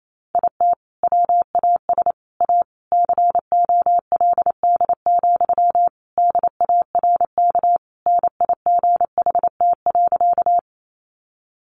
Morse code, heard as a keyed tone, about 28 wpm, 700 Hz. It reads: IT WAS A COLD, DARK NIGHT.
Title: Sentences from Top 500 Words 28wpm